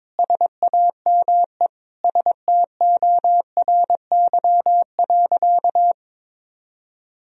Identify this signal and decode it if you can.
Morse code, heard as a keyed tone, about 22 wpm, 700 Hz. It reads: SAME STORY.
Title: Sentences from Top 1000 Words - Review Part 1 - 22wpm